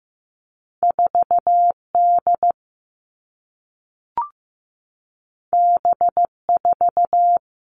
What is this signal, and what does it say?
Morse code, heard as a keyed tone, about 15 wpm, 700 Hz. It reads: 4D B4
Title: All Cut Numbers - 2-Digit 15wpm